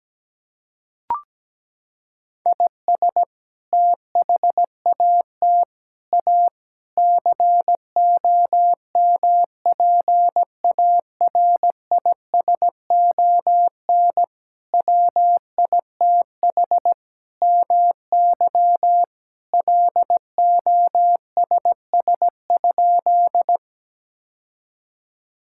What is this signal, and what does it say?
Morse code, heard as a keyed tone, about 17 wpm, 700 Hz. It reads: IS THAT A COMPARISON WITH MY LOSS?
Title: Sentences from Top 1600 Words 17wpm